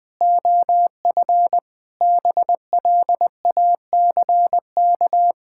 Morse code, heard as a keyed tone, about 20 wpm, 700 Hz. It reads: OF BLACK
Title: Pangram Challenge v2 - 20wpm